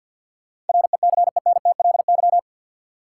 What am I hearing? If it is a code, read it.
Morse code, heard as a keyed tone, about 52 wpm, 700 Hz. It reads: RECENTLY
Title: Sentences from Top 1200 Words - New Word Drill - 50wpm